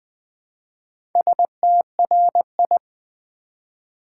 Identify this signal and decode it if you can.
Morse code, heard as a keyed tone, about 20 wpm, 700 Hz. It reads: STRI
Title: Build A Word - 1 Letter At A Time - 5-Letter Words - 20wpm